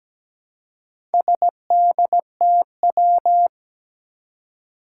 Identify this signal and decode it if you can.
Morse code, heard as a keyed tone, about 17 wpm, 700 Hz. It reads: SDTW
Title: Four Challenging Letter-Number Combinations 17wpm